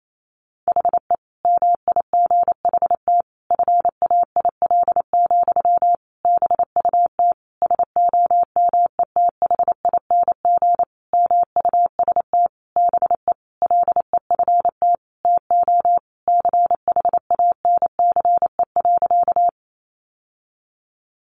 Morse code, heard as a keyed tone, about 28 wpm, 700 Hz. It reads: HE MIGHT FAIL, BUT SOMETHING MUST BE LEFT TO CHANCE.
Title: Sentences from Top 1100 Words 28wpm